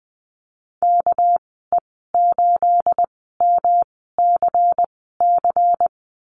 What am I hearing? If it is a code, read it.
Morse code, heard as a keyed tone, about 20 wpm, spaced as for 15 wpm, 700 Hz. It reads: KE8MCC
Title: 219 Lesson 35g - Number 0 - Call Signs 20wpm